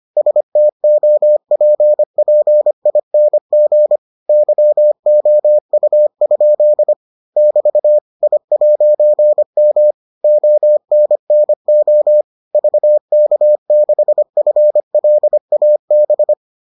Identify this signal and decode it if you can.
Morse code, heard as a keyed tone, about 25 wpm, 600 Hz. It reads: STOPPING YOU? <BT> I'M ONNO VK6FLAB